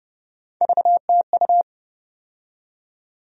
Morse code, heard as a keyed tone, about 30 wpm, 700 Hz. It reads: VTU
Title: USA Call Sign Postfixes - 3 Letters 30wpm